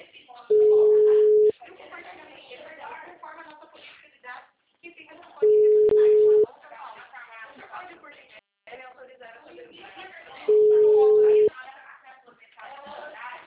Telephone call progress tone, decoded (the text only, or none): special information tone